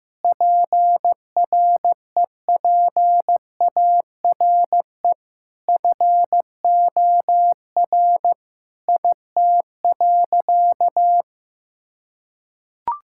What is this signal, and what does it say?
Morse code, heard as a keyed tone, about 15 wpm, 700 Hz. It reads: PREPARE FOR IT.
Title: Sentences from Top 1000 Words - Review Part 6 - 15wpm